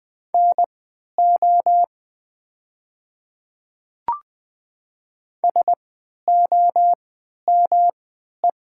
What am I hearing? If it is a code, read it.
Morse code, heard as a keyed tone, about 20 wpm, spaced as for 11 wpm, 700 Hz. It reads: NO SOME
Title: 3x Character Spacing - Top 100 Words - 20wpm